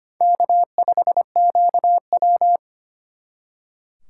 Morse code, heard as a keyed tone, about 32 wpm, 700 Hz. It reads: K5QW